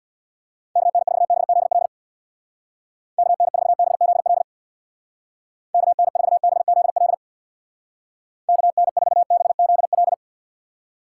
Morse code, heard as a keyed tone, about 65 wpm, 700 Hz. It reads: KN4BZL KN4BZL KN4BZL KN4BZL